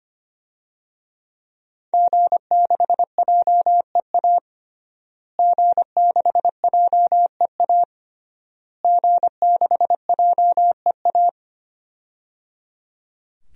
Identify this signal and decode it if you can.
Morse code, heard as a keyed tone, about 25 wpm, 700 Hz. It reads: G6JEA G6JEA G6JEA
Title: DXCC-100 Call Signs - 5 Characters - Repeated 3x - 25wpm